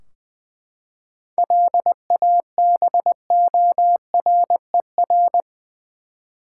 Morse code, heard as a keyed tone, about 20 wpm, 700 Hz. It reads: LABORER
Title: Practice QSO - Exchange 3 - More Info - 20wpm